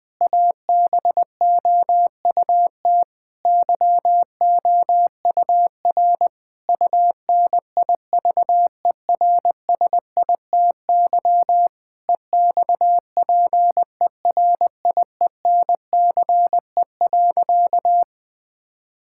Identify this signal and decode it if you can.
Morse code, heard as a keyed tone, about 20 wpm, 700 Hz. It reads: ABOUT YOUR UNIVERSITY EXPERIENCE.
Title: Sentences from Top 1000 Words - Review Part 5 - 20wpm